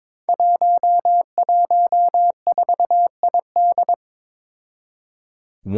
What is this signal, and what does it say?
Morse code, heard as a keyed tone, about 22 wpm, 700 Hz. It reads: 114ID
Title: Five Challenging Letter-Number Combinations 22wpm